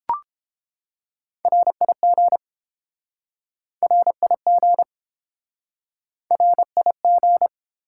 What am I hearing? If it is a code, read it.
Morse code, heard as a keyed tone, about 30 wpm, 690 Hz. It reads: RIG RIG RIG